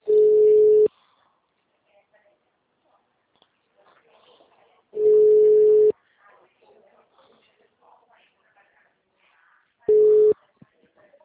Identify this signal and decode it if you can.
Ringback tone (Europe, 425 Hz), with speech.
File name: out- -2025- - - .332